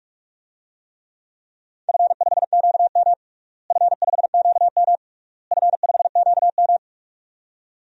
Morse code, heard as a keyed tone, about 46 wpm, 700 Hz. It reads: F5QM F5QM F5QM
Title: DXCC-100 Call Signs - 4 Characters - Repeated 3x - 45wpm